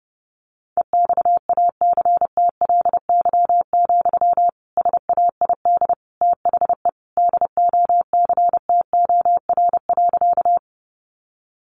Morse code, heard as a keyed tone, about 30 wpm, 700 Hz. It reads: EXACTLY, SAID THE DOCTOR.
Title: Sentences from Top 1500 Words 30wpm